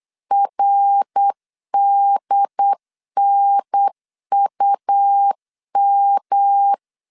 Morse code, heard as a keyed tone, about 8 wpm, 800 Hz. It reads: RDNUM
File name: LEZIONI / 6 PKX0 / 6 P 8wpm 3025